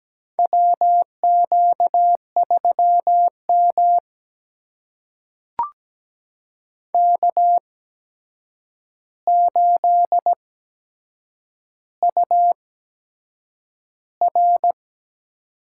Morse code, heard as a keyed tone, about 17 wpm, spaced as for 7 wpm, 700 Hz. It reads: WQ3M K8UR
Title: USA Call Signs - 4 Characters - 8x Character Spacing - 17wpm